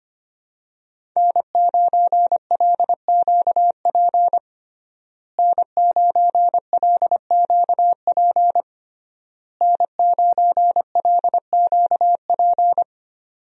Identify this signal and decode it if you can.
Morse code, heard as a keyed tone, about 25 wpm, 700 Hz. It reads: N9LQP N9LQP N9LQP
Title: USA Call Signs - 5 Characters - Repeated 3x - 25wpm